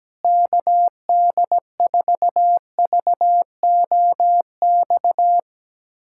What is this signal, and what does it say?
Morse code, heard as a keyed tone, about 17 wpm, 700 Hz. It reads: KD4VOX
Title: USA Callsign DE Callsign - 6 Characters - 17wpm